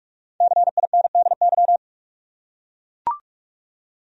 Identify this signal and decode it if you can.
Morse code, heard as a keyed tone, about 44 wpm, 700 Hz. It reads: CINDY
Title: USA Common Names - Max 5 Letters - 45wpm